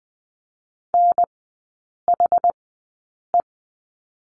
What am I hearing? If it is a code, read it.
Morse code, heard as a keyed tone, about 20 wpm, spaced as for 8 wpm, 700 Hz. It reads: NHE
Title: 191 Lesson 31e - Letter v - Three Characters 20wpm